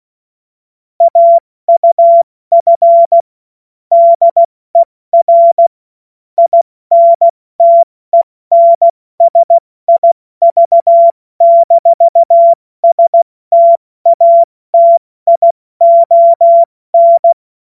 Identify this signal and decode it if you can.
Morse code, heard as a keyed tone, about 16 wpm, 680 Hz. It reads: AUF DER INTENSIV-STATION
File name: Morsekurs / Witze / Witz 22. Juni 2021, 70 F80